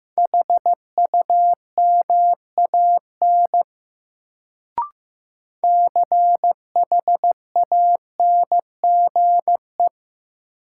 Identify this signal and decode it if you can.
Morse code, heard as a keyed tone, about 15 wpm, 700 Hz. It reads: HUMAN CHANGE
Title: Sentences from Top 600 Words - New Word Drill - 15wpm